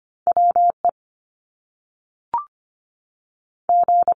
Morse code, heard as a keyed tone, about 25 wpm, 700 Hz. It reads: WE G
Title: Top Two Letter Combinations in Words 25wpm